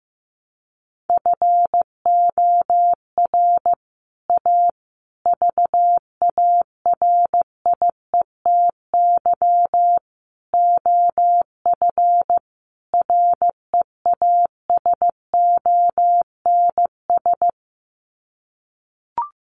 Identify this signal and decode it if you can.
Morse code, heard as a keyed tone, about 15 wpm, 700 Hz. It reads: FOR A VARIETY OF REASONS
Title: Sets of 5 Words Encore 15wpm